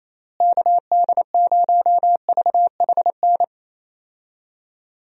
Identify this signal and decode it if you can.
Morse code, heard as a keyed tone, about 27 wpm, 700 Hz. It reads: KD0VHN
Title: USA Call Signs - 6 Characters - Repeated 2x - 28wpm